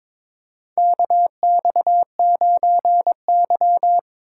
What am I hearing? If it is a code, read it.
Morse code, heard as a keyed tone, about 22 wpm, 700 Hz. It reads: KX9Y
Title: USA Call Signs - 4 Characters - 6x Character Spacing - 22wpm